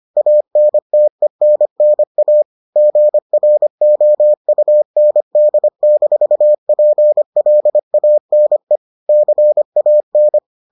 Morse code, heard as a keyed tone, about 25 wpm, 600 Hz. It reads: ANTENNA GROUND-PLANE CAN